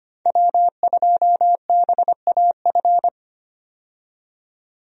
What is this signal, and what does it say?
Morse code, heard as a keyed tone, about 25 wpm, 700 Hz. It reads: W2BAF
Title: USA Call Signs - 5 Characters - 25wpm